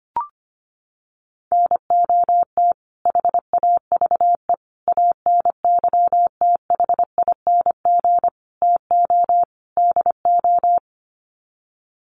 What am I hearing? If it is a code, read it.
Morse code, heard as a keyed tone, about 25 wpm, 700 Hz. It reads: NOT HAVE ANYTHING TO DO
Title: Sets of 5 Words Encore 25wpm